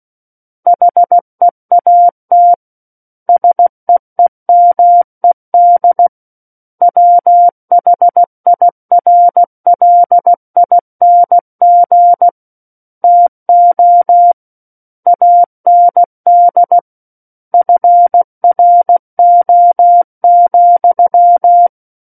Morse code, heard as a keyed tone, about 16 wpm, 700 Hz. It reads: HEAT SEEMED WHIRLING TO AND FRO,
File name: Book One / 07-Chapter 7 - 16 WPM